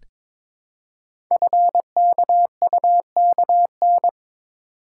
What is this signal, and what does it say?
Morse code, heard as a keyed tone, about 22 wpm, 700 Hz. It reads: FKUKN
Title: Code Groups - 5 Letters - 22wpm